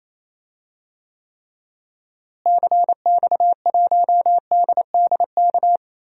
Morse code, heard as a keyed tone, about 28 wpm, 700 Hz. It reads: CX1DDK
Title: DXCC-100 Call Signs - 6 Characters - Repeated 3x - 28wpm